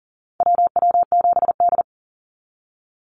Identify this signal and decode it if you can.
Morse code, heard as a keyed tone, about 42 wpm, 700 Hz. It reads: WW7D
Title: NA SOTA Activator Callsigns 40wpm